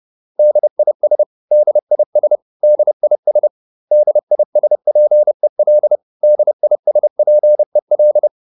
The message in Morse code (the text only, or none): DIS DIS DIS DISPEL DISPEL